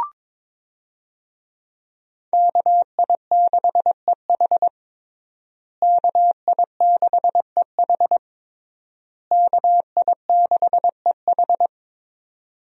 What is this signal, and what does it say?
Morse code, heard as a keyed tone, about 22 wpm, 700 Hz. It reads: KI6EH KI6EH KI6EH
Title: USA Call Signs - 5 Characters - Repeated 3x - 22wpm